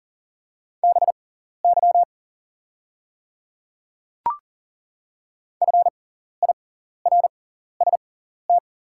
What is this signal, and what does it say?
Morse code, heard as a keyed tone, about 40 wpm, spaced as for 14 wpm, 700 Hz. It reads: BY FIRST
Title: 6x Character Spacing - Top 100 Words - 40wpm